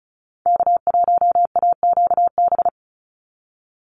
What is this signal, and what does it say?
Morse code, heard as a keyed tone, about 34 wpm, 700 Hz. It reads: K1AQB
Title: Call Signs Repeated 3x - 35wpm